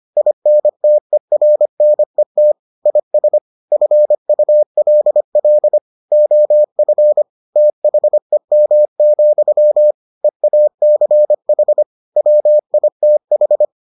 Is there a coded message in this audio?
Yes: Morse code from 0.2 to 13.7 s, heard as a keyed tone, about 25 wpm, 600 Hz. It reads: INTERNET IS FULL OF THEM, EACH WITH